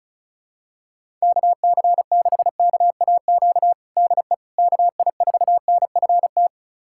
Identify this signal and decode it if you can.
Morse code, heard as a keyed tone, about 36 wpm, 700 Hz. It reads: KC6KAQ DE KI4NFT
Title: USA Callsign DE Callsign - 6 Characters - 35wpm